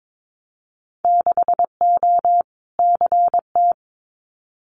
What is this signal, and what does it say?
Morse code, heard as a keyed tone, about 22 wpm, 700 Hz. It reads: 6O CT
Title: Winter Field Day Exchanges Encore 22wpm